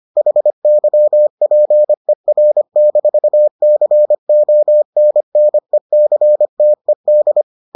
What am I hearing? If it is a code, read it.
Morse code, heard as a keyed tone, about 25 wpm, 600 Hz. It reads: HYPER-CONNECTED